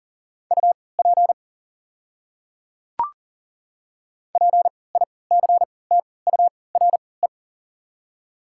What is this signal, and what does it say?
Morse code, heard as a keyed tone, about 40 wpm, spaced as for 23 wpm, 700 Hz. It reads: UP PICTURE
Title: 3x Character Spacing - Top 300 Words - 40wpm